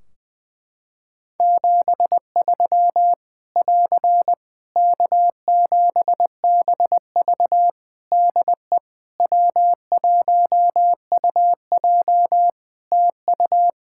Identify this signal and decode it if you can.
Morse code, heard as a keyed tone, about 20 wpm, 700 Hz. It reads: 73 <AR> K7BV DE W1UJ TU
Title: K1USN SST QSO Practice 20wpm